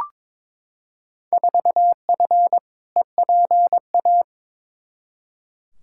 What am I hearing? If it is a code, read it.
Morse code, heard as a keyed tone, about 22 wpm, 700 Hz. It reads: 4F EPA
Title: Field Day Exchanges - 22wpm